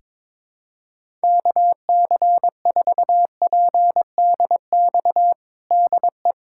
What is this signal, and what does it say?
Morse code, heard as a keyed tone, about 22 wpm, 700 Hz. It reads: KC4PDX DE
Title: USA Callsign DE Callsign 22wpm